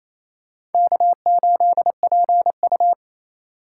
Morse code, heard as a keyed tone, about 27 wpm, 700 Hz. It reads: K8PU
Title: USA Call Signs - 2x Character Spacing - 28wpm